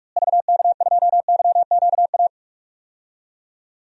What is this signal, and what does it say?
Morse code, heard as a keyed tone, about 45 wpm, 700 Hz. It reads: VK2YQA
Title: DXCC-100 Call Signs - 6 Characters - Repeated 2x - 45wpm